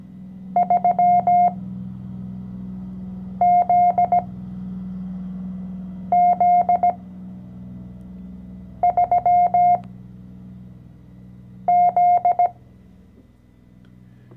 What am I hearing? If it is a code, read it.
Morse code, heard as a keyed tone, about 17 wpm, spaced as for 4 wpm, 710 Hz. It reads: 3ZZ3Z
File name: CW-Online-Kurs / Anfänger / Lektion 014